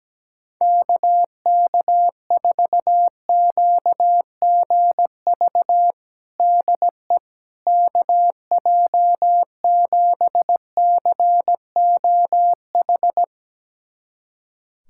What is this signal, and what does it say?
Morse code, heard as a keyed tone, about 17 wpm, 700 Hz. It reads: KK4QGV DE KJ7COH
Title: USA Callsign DE Callsign 17wpm